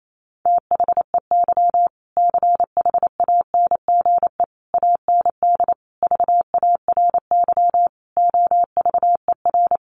Morse code, heard as a keyed tone, about 28 wpm, 700 Hz. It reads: THEY CHANGE AND VARY OVER
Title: Sentences from Top 1200 Words 28wpm